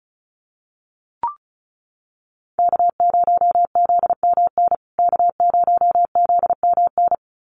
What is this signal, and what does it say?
Morse code, heard as a keyed tone, about 34 wpm, 700 Hz. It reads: K0ZMN K0ZMN
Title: USA Call Signs Repeated 2x 35wpm